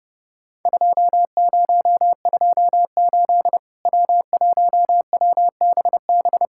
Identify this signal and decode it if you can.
Morse code, heard as a keyed tone, about 28 wpm, 700 Hz. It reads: 2028 W1WBB
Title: Worked All Europe DX Contest - QTC Exchange - 30wpm